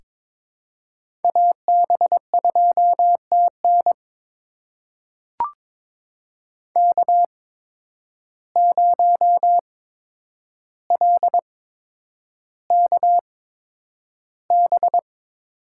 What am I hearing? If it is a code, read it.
Morse code, heard as a keyed tone, about 22 wpm, spaced as for 9 wpm, 700 Hz. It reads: AB2TN K0LKB
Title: USA Call Signs - 8x Character Spacing - 22wpm